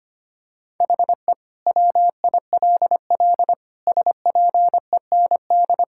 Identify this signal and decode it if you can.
Morse code, heard as a keyed tone, about 25 wpm, 700 Hz. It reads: HE WILL SPEND
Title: Sentences from Top 1000 Words - Review Part 5 - 25wpm